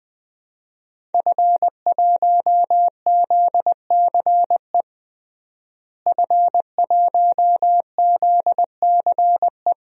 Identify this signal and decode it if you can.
Morse code, heard as a keyed tone, about 20 wpm, 700 Hz. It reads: F1ZCE F1ZCE
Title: DXCC-100 Call Signs - 5 Characters - Repeated 2x - 20wpm